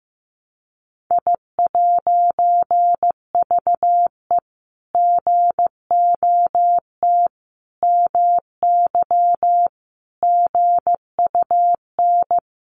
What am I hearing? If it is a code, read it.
Morse code, heard as a keyed tone, about 15 wpm, 700 Hz. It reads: I'VE GOT MY GUN
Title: A Sound Of Thunder 15wpm